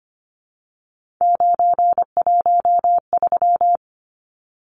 Morse code, heard as a keyed tone, about 25 wpm, 700 Hz. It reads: 913
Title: Numbers - 3-Digit 25wpm